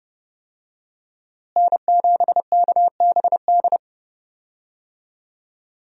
Morse code, heard as a keyed tone, about 32 wpm, 700 Hz. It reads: N7KBD
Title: USA Call Signs - 5 Characters - Repeated 3x - 30wpm